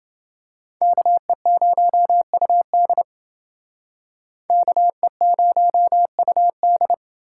Morse code, heard as a keyed tone, about 28 wpm, 700 Hz. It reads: KE0UD KE0UD